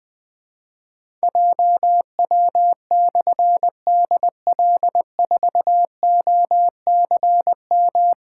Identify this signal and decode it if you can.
Morse code, heard as a keyed tone, about 20 wpm, 700 Hz. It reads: JW/DL4OCM
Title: Difficult Call Signs 20wpm